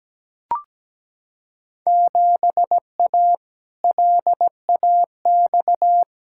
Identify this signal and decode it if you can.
Morse code, heard as a keyed tone, about 17 wpm, 700 Hz. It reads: 7A LAX
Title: Field Day Exchanges - 17wpm